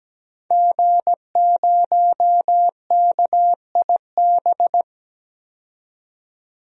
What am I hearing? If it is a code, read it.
Morse code, heard as a keyed tone, about 17 wpm, 700 Hz. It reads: G0KIB
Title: DXCC-100 Call Signs - 5 Characters - 17wpm